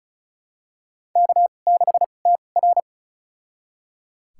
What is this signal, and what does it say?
Morse code, heard as a keyed tone, about 36 wpm, spaced as for 25 wpm, 700 Hz. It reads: K6TR
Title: USA Call Signs - 4 Characters - 2x Character Spacing - 35wpm